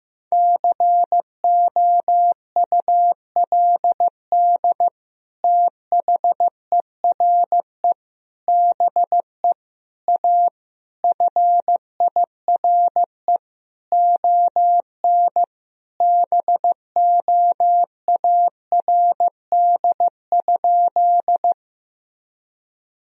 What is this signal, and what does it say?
Morse code, heard as a keyed tone, about 15 wpm, 700 Hz. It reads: COULD THERE BE A FIRE ON BOARD?
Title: Sentences from Top 1000 Words - Review Part 4 - 15wpm